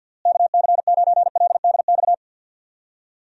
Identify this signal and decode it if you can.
Morse code, heard as a keyed tone, about 52 wpm, 700 Hz. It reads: KC9LDX
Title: USA Call Signs - 6 Characters - Repeated 3x - 50wpm